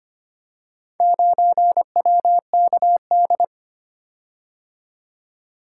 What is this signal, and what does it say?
Morse code, heard as a keyed tone, about 25 wpm, 700 Hz. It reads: 9WKD
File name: practice / 25wpm / Four Letter-Number Combinations - 25wpm